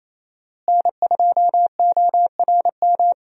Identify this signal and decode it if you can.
Morse code, heard as a keyed tone, about 28 wpm, 700 Hz. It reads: N2ORM
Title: USA Call Signs - 5 Characters - Repeated 3x - 28wpm